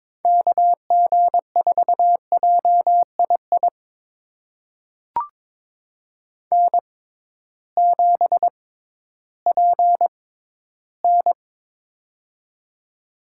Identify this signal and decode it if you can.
Morse code, heard as a keyed tone, about 22 wpm, spaced as for 13 wpm, 700 Hz. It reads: KG4JII N7PN
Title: USA Call Signs - 6x Character Spacing - 22wpm